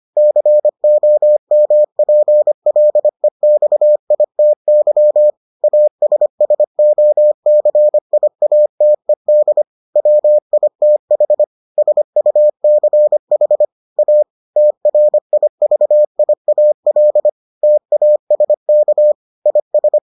Morse code, heard as a keyed tone, about 25 wpm, 600 Hz. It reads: COMPLEXITY ASSOCIATED WITH SUCH A TRIVIAL TASK IS